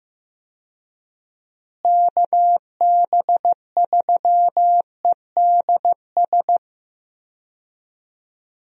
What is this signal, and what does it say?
Morse code, heard as a keyed tone, about 15 wpm, 700 Hz. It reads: KB3EDS
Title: USA Call Signs - 6 Characters - Repeated 3x - 15wpm